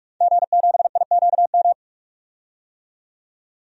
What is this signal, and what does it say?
Morse code, heard as a keyed tone, about 43 wpm, 700 Hz. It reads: G7IQM
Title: DXCC-100 Call Signs - 5 Characters - 45wpm